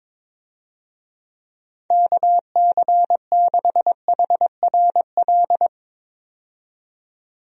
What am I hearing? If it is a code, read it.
Morse code, heard as a keyed tone, about 22 wpm, 700 Hz. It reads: KC6HRL
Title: USA Call Signs - 6 Characters - 22wpm